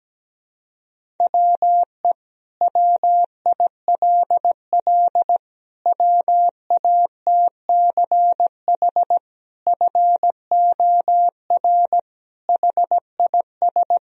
Morse code, heard as a keyed tone, about 17 wpm, 700 Hz. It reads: WE WILL WATCH FOR HIS
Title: Sentences from Top 1000 Words - Review Part 1 - 17wpm